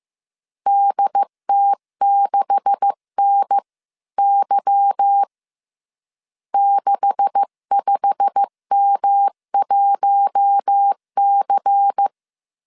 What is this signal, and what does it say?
Morse code, heard as a keyed tone, about 15 wpm, 800 Hz. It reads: DT6NY 65M1C
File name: CW / DET / 1 9 1500C S B 14wpm 3302